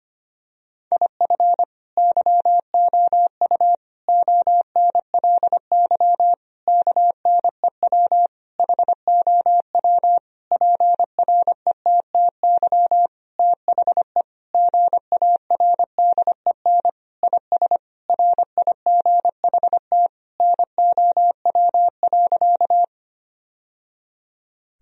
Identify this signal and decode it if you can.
Morse code, heard as a keyed tone, about 25 wpm, 700 Hz. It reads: IF YOU ONLY KNEW HOW PRETTY THE GARDEN IS RIGHT NOW.
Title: Sentences from Top 1000 Words - Review Part 4 - 25wpm